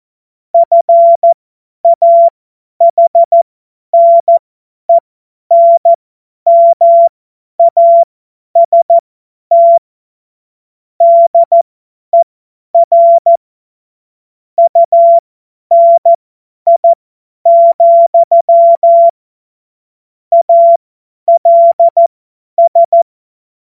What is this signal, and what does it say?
Morse code, heard as a keyed tone, about 14 wpm, spaced as for 10 wpm, 680 Hz. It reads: FAHNENMAST DER UNI, ALS
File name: Morsekurs / Witze / Witz 21. Juni 2022 50 F70